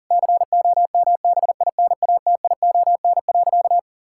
Morse code, heard as a keyed tone, about 40 wpm, 700 Hz. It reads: COMBINATION.
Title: Sentences from Top 1000 Words - Review Part 6 - 40wpm